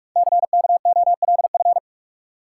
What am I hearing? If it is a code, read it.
Morse code, heard as a keyed tone, about 44 wpm, 700 Hz. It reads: CKOLF
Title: Code Groups - 5 Letters - 45wpm